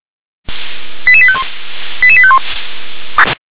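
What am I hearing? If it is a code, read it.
Five-tone selective call (ZVEI), heard as quick runs of five tones, tones of about 70 ms. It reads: 88851 88851
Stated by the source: ZVEI five-tone selective call received on SDR.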